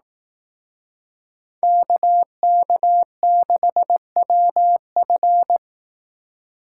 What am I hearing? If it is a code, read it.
Morse code, heard as a keyed tone, about 18 wpm, 700 Hz. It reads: KK6WF
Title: Speed Racing - USA Call Signs 15wpm